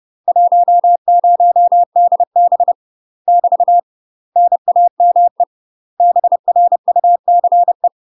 Morse code, heard as a keyed tone, about 30 wpm, 700 Hz. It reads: 10DB <BT> NAME BRUCE
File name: UK MASTER QSOs TRAINING / 30WPM / UK MASTER QSOs 0001